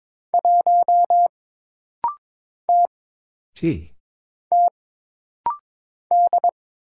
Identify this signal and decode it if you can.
Morse code, heard as a keyed tone, about 22 wpm, 700 Hz. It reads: 1 T T D
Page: This rapid-fire timing is geared to push your ICR (Instant Character Recognition) skills.